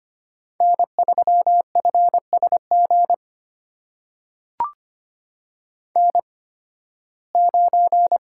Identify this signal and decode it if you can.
Morse code, heard as a keyed tone, about 25 wpm, spaced as for 16 wpm, 700 Hz. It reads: N3FSG N9
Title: USA Call Signs - 5 Characters - 8x Character Spacing - 25wpm